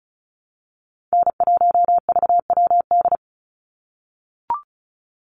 Morse code, heard as a keyed tone, about 34 wpm, 700 Hz. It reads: N1VWD
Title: NA SOTA Chaser Callsigns 35wpm